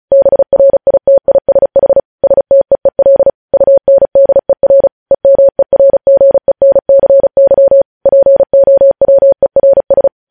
Morse code, heard as a keyed tone, about 36 wpm, 550 Hz. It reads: BRITISH STEEL UNDER EMERGENCY POWERS
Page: News Headlines in Morse Code at 35 WPM